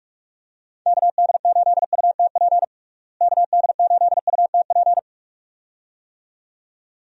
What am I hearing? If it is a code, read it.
Morse code, heard as a keyed tone, about 42 wpm, 700 Hz. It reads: KD8UTP KD8UTP
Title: USA Call Signs - 6 Characters - Repeated 2x - 45wpm